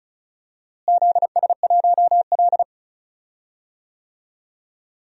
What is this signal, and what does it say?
Morse code, heard as a keyed tone, about 34 wpm, 700 Hz. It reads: ZS1L
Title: DXCC-100 Call Signs - 4 Characters - Repeated 3x - 35wpm